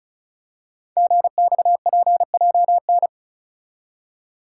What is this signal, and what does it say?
Morse code, heard as a keyed tone, about 36 wpm, 690 Hz. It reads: GXPJN
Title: Code Groups - 5 Letter:Numbers - 35wpm